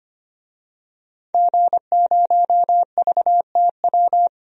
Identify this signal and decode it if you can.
Morse code, heard as a keyed tone, about 25 wpm, 700 Hz. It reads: G0VTW
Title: DXCC-100 Call Signs - 5 Characters - 25wpm